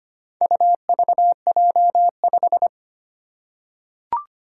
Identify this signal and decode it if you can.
Morse code, heard as a keyed tone, about 25 wpm, 700 Hz. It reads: UVJ5
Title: Four Challenging Letter-Number Combinations 25wpm